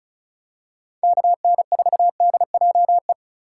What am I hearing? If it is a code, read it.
Morse code, heard as a keyed tone, about 34 wpm, 700 Hz. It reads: KN4DJE
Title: USA Call Signs - 6 Characters - 3x Character Spacing - 35wpm